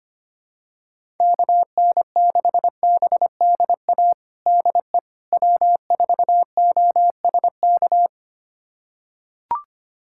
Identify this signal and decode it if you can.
Morse code, heard as a keyed tone, about 25 wpm, 700 Hz. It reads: KN6BDA DE W4OSK
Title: USA Callsign DE Callsign 25wpm